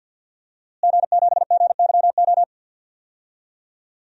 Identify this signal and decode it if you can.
Morse code, heard as a keyed tone, about 51 wpm, 700 Hz. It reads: G7GYO
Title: DXCC-100 Call Signs - 5 Characters - Repeated 2x - 50wpm